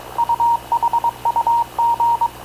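Morse code, heard as a keyed tone, about 22 wpm, 940 Hz. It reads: UHUG
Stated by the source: Havana Radio / CLA sending Spanish press (PX) in Morse, recorded by Radio Officer Paul Dumesnil onboard CNAV QUEST/CZDO.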